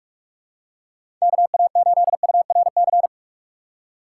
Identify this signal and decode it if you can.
Morse code, heard as a keyed tone, about 46 wpm, 700 Hz. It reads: KA8URC
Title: USA Call Signs - 6 Characters - 2x Character Spacing - 45wpm